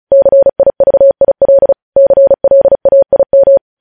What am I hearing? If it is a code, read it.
Morse code, heard as a keyed tone, about 36 wpm, 550 Hz. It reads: CIVIL CLAIM
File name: podcast / News headlines-1x-35x35